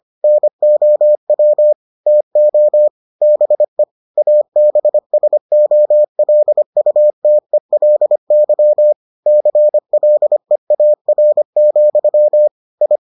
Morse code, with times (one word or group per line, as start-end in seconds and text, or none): 0.2-1.7 NOW
2.1-2.9 TO
3.2-3.8 BE
4.2-8.9 ABSOLUTELY
9.3-12.5 CLEAR,
12.8-13.0 I